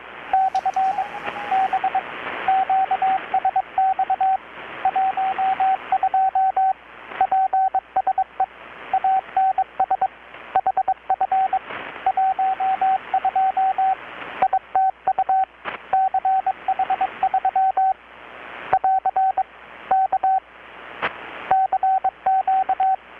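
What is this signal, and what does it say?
Morse code, heard as a keyed tone, about 21 wpm, 760 Hz, with radio noise. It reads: / B QSX 12 PSE ANS HF 12 ITU CH3 <AR> K CQ
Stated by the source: Morse from KPH's Night of Nights XX maritime radiotelegraphy event, received in Chiapas, Mexico about 3,800 km away on a Kenwood R-600 with a grounded 10 m long-wire antenna.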